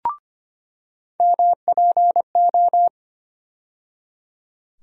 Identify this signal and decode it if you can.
Morse code, heard as a keyed tone, about 25 wpm, 700 Hz. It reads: MPO
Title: USA Call Sign Postfixes - 3 Letters 25wpm